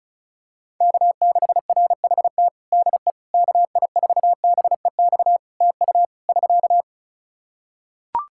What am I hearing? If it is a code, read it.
Morse code, heard as a keyed tone, about 34 wpm, 700 Hz. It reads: K6RHT DE KI4BEX TU <SK>
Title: K1USN SST QSO Practice 35wpm